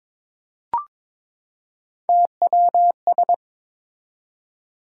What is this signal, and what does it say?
Morse code, heard as a keyed tone, about 21 wpm, 700 Hz. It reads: TWS
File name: practice / 22wpm / Three Letter-Number Combinations - 22wpm